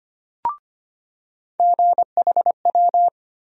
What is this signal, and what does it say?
Morse code, heard as a keyed tone, about 25 wpm, 700 Hz. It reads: GHW
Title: USA Call Sign Postfixes - 3 Letters 25wpm